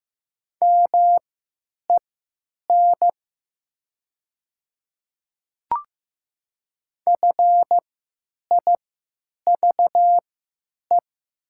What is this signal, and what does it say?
Morse code, heard as a keyed tone, about 15 wpm, spaced as for 9 wpm, 700 Hz. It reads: MEN FIVE
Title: 3x Character Spacing - Top 500 Words - 15wpm